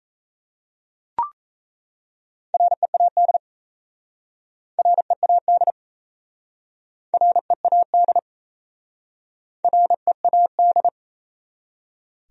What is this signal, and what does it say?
Morse code, heard as a keyed tone, about 37 wpm, 700 Hz. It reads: READ READ READ READ